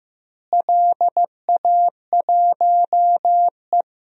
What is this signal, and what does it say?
Morse code, heard as a keyed tone, about 15 wpm, 700 Hz. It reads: LA1E
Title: DXCC-100 Call Signs - 4 Characters - Repeated 3x - 15wpm